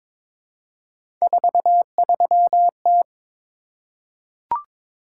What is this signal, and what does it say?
Morse code, heard as a keyed tone, about 22 wpm, 700 Hz. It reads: 43T
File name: practice / 22wpm / Three Letter-Number Combinations - 22wpm